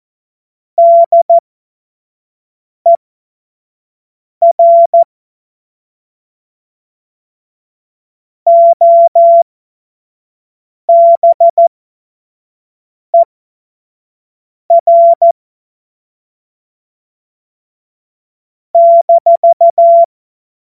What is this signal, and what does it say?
Morse code, heard as a keyed tone, about 14 wpm, spaced as for 5 wpm, 680 Hz. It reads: DER OBER -
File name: Morsekurs / Witze / Witz 04. Mai 2021, 25 F70